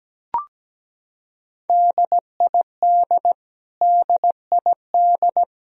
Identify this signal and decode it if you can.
Morse code, heard as a keyed tone, about 17 wpm, 700 Hz. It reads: DID DID
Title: Top 100 Words Repeated 17wpm